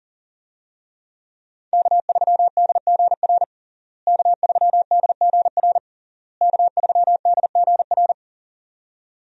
Morse code, heard as a keyed tone, about 40 wpm, 700 Hz. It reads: K3DGR K3DGR K3DGR
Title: USA Call Signs - 5 Characters - Repeated 3x - 40wpm